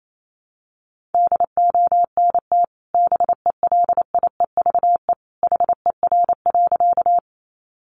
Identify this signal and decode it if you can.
Morse code, heard as a keyed tone, about 28 wpm, 700 Hz. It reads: DONT BELIEVE HER.
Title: Sentences from Top 1000 Words 28wpm